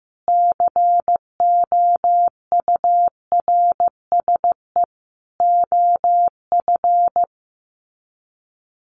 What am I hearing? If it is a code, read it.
Morse code, heard as a keyed tone, about 15 wpm, 700 Hz. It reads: COURSE OF
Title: Sets of 4 Words 15wpm